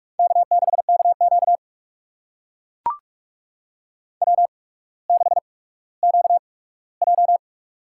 Morse code, heard as a keyed tone, about 44 wpm, spaced as for 19 wpm, 700 Hz. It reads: K6KQ W6QJ
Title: USA Call Signs - 4 Characters - 8x Character Spacing - 45wpm